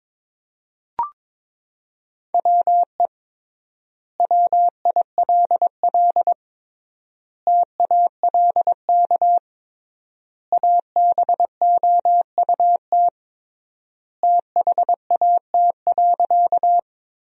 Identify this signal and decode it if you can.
Morse code, heard as a keyed tone, about 22 wpm, 700 Hz. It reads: WE WILL TALK ABOUT THAT.
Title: 3x Word-Spacing - Sentences from Top 400 Words - 22wpm